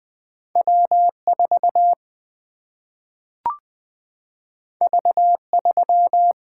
Morse code, heard as a keyed tone, about 20 wpm, 700 Hz. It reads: W4 V3
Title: Single-Letter Single-Number 20wpm